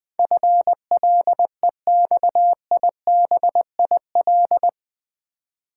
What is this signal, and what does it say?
Morse code, heard as a keyed tone, about 20 wpm, 700 Hz. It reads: FLEXIBIL
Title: Build A Word - 4 Letters At A Time - 11-Letter Words - 20wpm